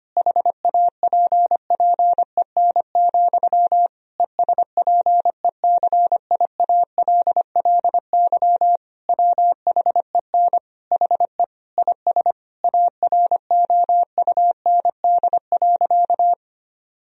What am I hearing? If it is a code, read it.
Morse code, heard as a keyed tone, about 25 wpm, 700 Hz. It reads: HAPPEN, ESPECIALLY WHEN HE IS AROUND.
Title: Sentences from Top 1000 Words - Review Part 4 - 25wpm